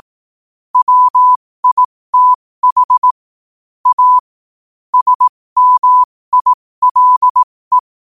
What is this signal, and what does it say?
Morse code, heard as a keyed tone, about 18 wpm, spaced as for 15 wpm, 1000 Hz. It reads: WITH A SMILE
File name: qotdfiles / 15wpm